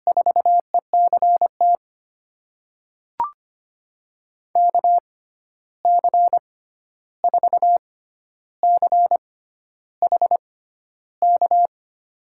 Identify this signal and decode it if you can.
Morse code, heard as a keyed tone, about 25 wpm, spaced as for 11 wpm, 700 Hz. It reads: 4ECT KC4CHK
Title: USA Call Signs - 6 Characters - 6x Character Spacing - 25wpm